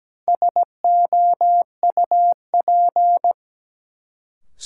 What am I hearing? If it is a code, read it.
Morse code, heard as a keyed tone, about 17 wpm, 700 Hz. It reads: SOUP
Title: Sentences from Top 1600 Words - New Word Drill Alt - 17wpm